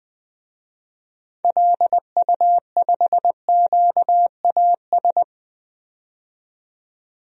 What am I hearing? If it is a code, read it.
Morse code, heard as a keyed tone, about 20 wpm, 700 Hz. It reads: LU5QAS
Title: DXCC-100 Call Signs - 6 Characters - 20wpm